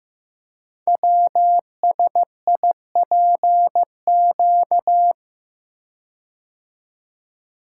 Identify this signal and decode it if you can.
Morse code, heard as a keyed tone, about 15 wpm, 700 Hz. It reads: WSIPQ
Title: Code Groups - 5 Letters - 15wpm